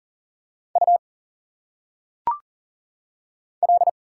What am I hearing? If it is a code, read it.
Morse code, heard as a keyed tone, about 40 wpm, 700 Hz. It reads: U L
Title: Single Letters - Phonetic Pronunciation 40wpm